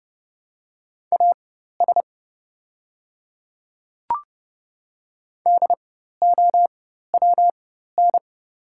Morse code, heard as a keyed tone, about 30 wpm, spaced as for 14 wpm, 700 Hz. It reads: AS DOWN
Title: 4x Character Spacing - Top 100 Words - 30wpm